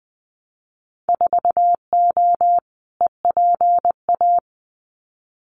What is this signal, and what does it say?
Morse code, heard as a keyed tone, about 19 wpm, 700 Hz. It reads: 4O EPA
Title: Winter Field Day Exchanges 20wpm